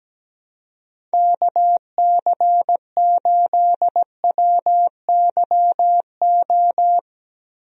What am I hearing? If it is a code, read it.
Morse code, heard as a keyed tone, about 17 wpm, 700 Hz. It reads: KC8WYO
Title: USA Call Signs - 6 Characters - Repeated 2x - 17wpm